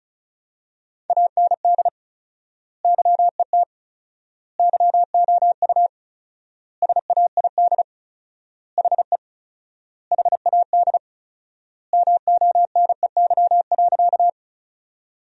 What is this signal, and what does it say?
Morse code, heard as a keyed tone, about 36 wpm, 700 Hz. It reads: AND YET YOU SAID HE HAD MONEY.
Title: 4x Word-Spacing - Sentences from Top 500 Words - 35wpm